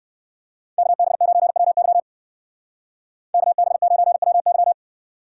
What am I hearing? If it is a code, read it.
Morse code, heard as a keyed tone, about 55 wpm, 700 Hz. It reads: KB9WY KB9WY
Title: Speed Racing - USA 2x2 Call Signs 45wpm